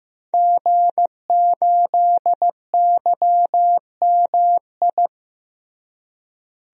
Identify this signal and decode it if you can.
Morse code, heard as a keyed tone, about 14 wpm, 700 Hz. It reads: G8YMI
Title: DXCC-100 Call Signs - 5 Characters - Repeated 3x - 15wpm